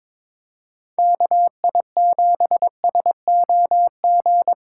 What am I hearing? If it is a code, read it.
Morse code, heard as a keyed tone, about 25 wpm, 700 Hz. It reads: KI7SOG
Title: Speed Racing - USA 2x3 Call Signs 22wpm